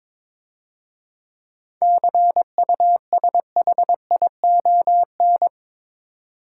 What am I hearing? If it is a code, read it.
Morse code, heard as a keyed tone, about 23 wpm, 700 Hz. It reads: CUSHION
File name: Easy Words - Speed-Racing with Hints - 22wpm